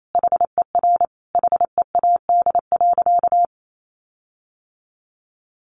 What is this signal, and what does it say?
Morse code, heard as a keyed tone, about 28 wpm, 700 Hz. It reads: HER HEAD.
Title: Sentences from Top 1400 Words 28wpm